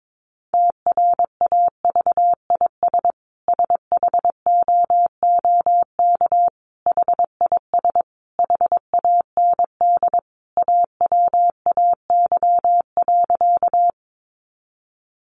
Morse code, heard as a keyed tone, about 22 wpm, 700 Hz. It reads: TRAVIS SHOOK HIS HAND AWAY.
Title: A Sound Of Thunder 22wpm